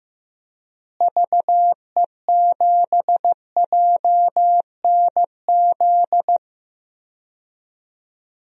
Morse code, heard as a keyed tone, about 15 wpm, 700 Hz. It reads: VE7JNZ
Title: DXCC-100 Call Signs - 6 Characters - Repeated 2x - 15wpm